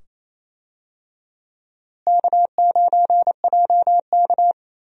Morse code, heard as a keyed tone, about 28 wpm, 700 Hz. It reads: K9JK
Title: USA Call Signs - 4 Characters - Repeated 2x - 28wpm